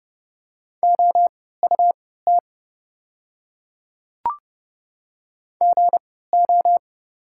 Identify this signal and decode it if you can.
Morse code, heard as a keyed tone, about 30 wpm, spaced as for 17 wpm, 700 Hz. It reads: OUT GO
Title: 3x Character Spacing - Top 100 Words - 30wpm